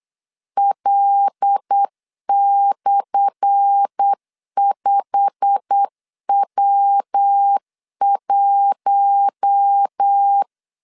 Morse code, heard as a keyed tone, about 8 wpm, 800 Hz. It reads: L/5W1
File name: LEZIONI / 10 SIGNS / 10A SLASH 8wpm 3452